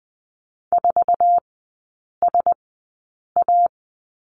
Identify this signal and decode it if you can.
Morse code, heard as a keyed tone, about 20 wpm, spaced as for 8 wpm, 700 Hz. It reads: 4SA
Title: 047 Lesson 9e - Number 4 - Three Characters 20wpm